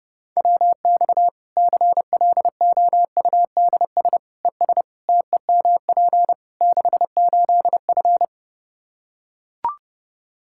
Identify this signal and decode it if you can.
Morse code, heard as a keyed tone, about 29 wpm, 700 Hz. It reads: WX CLOUDS ES TEMP 68F
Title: Practice QSO - Exchange 2 - Info - 30wpm